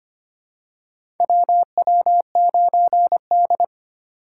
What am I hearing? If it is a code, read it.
Morse code, heard as a keyed tone, about 25 wpm, 700 Hz. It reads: WW9D
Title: USA Call Signs - 4 Characters - 8x Character Spacing - 25wpm